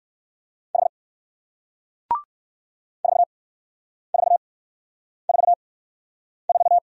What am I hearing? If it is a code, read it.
Morse code, heard as a keyed tone, about 56 wpm, spaced as for 9 wpm, 700 Hz. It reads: H 4444
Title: Speed-Racing Single Letter-Number 45wpm